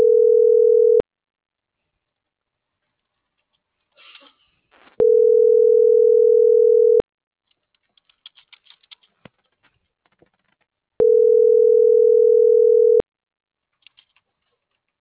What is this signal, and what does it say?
Ringback tone (North America), 440 + 480 Hz.